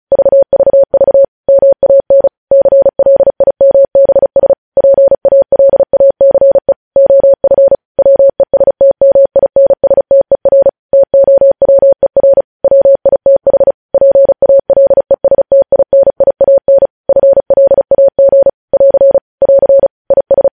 Morse code, heard as a keyed tone, about 36 wpm, 550 Hz. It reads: VVV MAN CLIMBS PALACE OF WESTMINSTER TOWER WITH PALESTINIAN FLAG <AR> <AR> IS
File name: podcast / News headlines-1x-35x35